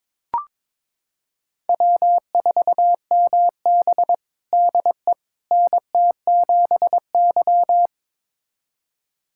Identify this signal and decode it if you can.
Morse code, heard as a keyed tone, about 22 wpm, 700 Hz. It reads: W4MB DE NT7Y
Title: USA Callsign DE Callsign - 4 Characters - 22wpm